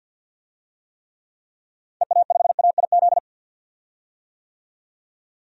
Morse code, heard as a keyed tone, about 51 wpm, 700 Hz. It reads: EA5AIZ
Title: DXCC-100 Call Signs - 6 Characters - 50wpm